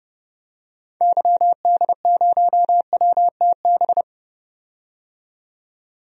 Morse code, heard as a keyed tone, about 32 wpm, 700 Hz. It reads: YD0WTB
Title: DXCC-100 Call Signs - 6 Characters - 30wpm